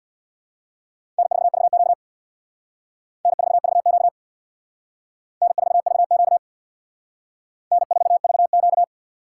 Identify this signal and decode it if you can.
Morse code, heard as a keyed tone, about 61 wpm, 700 Hz. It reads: N4VQ N4VQ N4VQ N4VQ
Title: Speed Racing - USA 1x2 Call Signs 50wpm